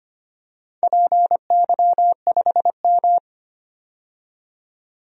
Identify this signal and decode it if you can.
Morse code, heard as a keyed tone, about 25 wpm, 700 Hz. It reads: PY5M
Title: DXCC-100 Call Signs - 4 Characters - 25wpm